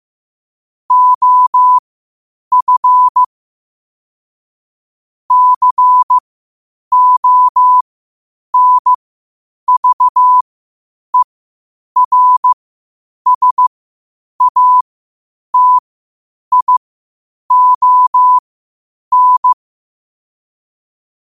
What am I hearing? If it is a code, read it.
Morse code, heard as a keyed tone, about 15 wpm, spaced as for 8 wpm, 1000 Hz. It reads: OF CONVERSATION
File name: qotdfiles / 8wpm